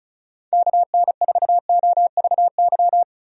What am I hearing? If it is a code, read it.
Morse code, heard as a keyed tone, about 34 wpm, 700 Hz. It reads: KN4OVY
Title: USA Call Signs - 6 Characters - Repeated 3x - 35wpm